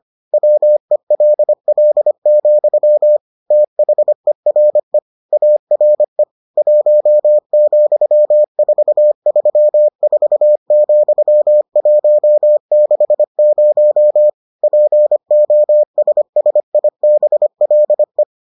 Morse code, heard as a keyed tone, about 25 wpm, 600 Hz. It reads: WELL, THERE ARE 1,434,160 POSSIBLE